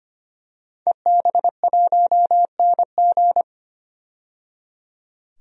Morse code, heard as a keyed tone, about 25 wpm, 700 Hz. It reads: EB1NG
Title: DXCC-100 Call Signs - 5 Characters - 25wpm